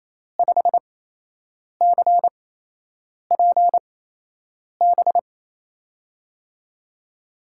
Morse code, heard as a keyed tone, about 28 wpm, spaced as for 8 wpm, 700 Hz. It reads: 5CPB
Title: USA Call Signs - 6 Characters - 8x Character Spacing - 28wpm